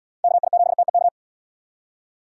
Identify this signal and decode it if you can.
Morse code, heard as a keyed tone, about 69 wpm, 700 Hz. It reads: KI7TEK